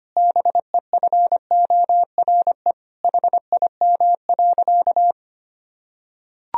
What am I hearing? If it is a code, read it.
Morse code, heard as a keyed tone, about 25 wpm, 700 Hz. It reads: BEFORE HIM.
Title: Sentences from Top 1000 Words - Review Part 2 - 25wpm